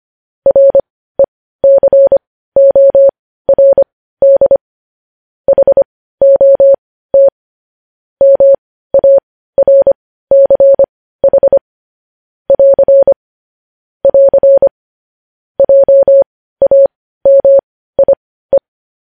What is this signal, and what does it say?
Morse code, heard as a keyed tone, about 25 wpm, spaced as for 15 wpm, 550 Hz. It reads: RECORD HOT MARCH <AR> <AR> JAMIE